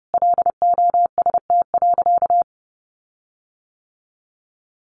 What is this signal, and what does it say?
Morse code, heard as a keyed tone, about 30 wpm, 700 Hz. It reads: LOST.
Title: Sentences from Top 1400 Words 30wpm